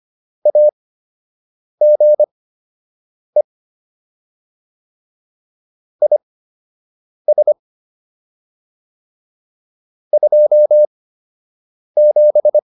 Morse code, heard as a keyed tone, about 25 wpm, spaced as for 7 wpm, 600 Hz. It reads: AGE IS 27